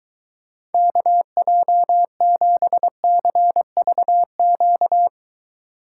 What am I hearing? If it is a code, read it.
Morse code, heard as a keyed tone, about 23 wpm, 700 Hz. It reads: KJ7CVQ